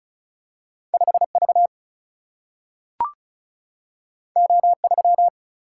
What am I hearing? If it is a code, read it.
Morse code, heard as a keyed tone, about 36 wpm, 700 Hz. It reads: 5V O3
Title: Single-Letter Single-Number 35wpm